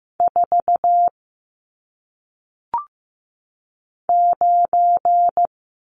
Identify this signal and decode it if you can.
Morse code, heard as a keyed tone, about 15 wpm, 700 Hz. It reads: 4 9
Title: Numbers - 1-Digit 15wpm